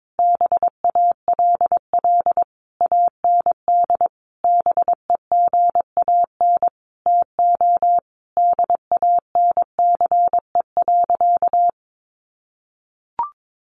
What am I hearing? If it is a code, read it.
Morse code, heard as a keyed tone, about 22 wpm, 700 Hz. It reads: BALL AND BEGAN TO DANCE.
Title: Sentences from Top 900 Words 22wpm